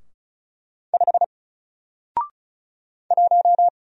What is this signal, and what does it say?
Morse code, heard as a keyed tone, about 36 wpm, 700 Hz. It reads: 5 1
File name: Single-Letter-Number-35wpm